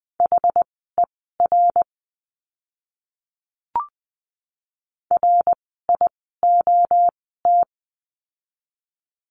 Morse code, heard as a keyed tone, about 20 wpm, spaced as for 14 wpm, 700 Hz. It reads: HER RIOT
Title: 061 Lesson 11f - Letter h - Words 20wpm